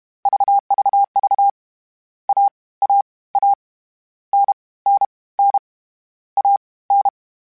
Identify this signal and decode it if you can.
Morse code, heard as a keyed tone, about 32 wpm, spaced as for 21 wpm, 800 Hz. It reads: VVV AAA NNN AN
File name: Short Runs / Short Runs-mp3s / Unballanced @30